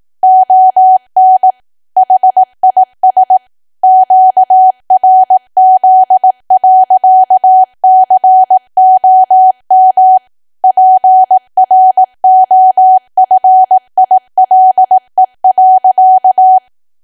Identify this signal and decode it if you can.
Morse code, heard as a keyed tone, about 18 wpm, 750 Hz. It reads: ON HIS QRZ.COM PROFILE.
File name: Archive / 18 WPM / 18WPM